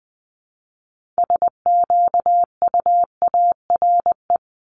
Morse code, heard as a keyed tone, about 20 wpm, 700 Hz. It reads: SQUARE
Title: Multi-letter Phonemes 20wpm